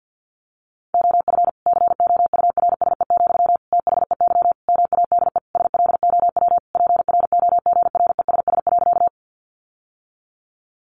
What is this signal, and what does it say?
Morse code, heard as a keyed tone, about 50 wpm, 700 Hz. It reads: OF COURSE, THEY MADE SLOW PROGRESS.
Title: Sentences from Top 1400 Words 50wpm